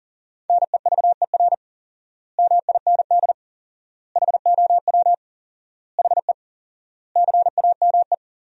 Morse code, heard as a keyed tone, about 40 wpm, 700 Hz. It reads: NEVER MIND HOW HE CAME
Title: 4x Word-Spacing - Sentences from Top 500 Words - 40wpm